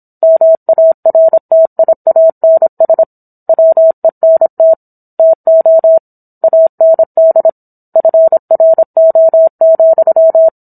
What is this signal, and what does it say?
Morse code, heard as a keyed tone, about 26 wpm, 650 Hz. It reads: MARTIANS WENT TO AND FRO,